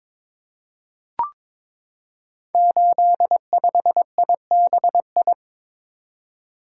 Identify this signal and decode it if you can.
Morse code, heard as a keyed tone, about 22 wpm, 700 Hz. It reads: 85IBI
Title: Code Groups - 5 Letter:Numbers - 22wpm